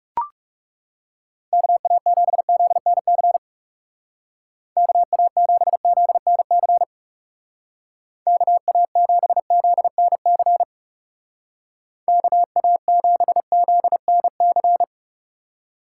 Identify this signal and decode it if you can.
Morse code, heard as a keyed tone, about 37 wpm, 700 Hz. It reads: KA7ZNC KA7ZNC KA7ZNC KA7ZNC
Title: Speed Racing - USA 2x3 Call Signs 30wpm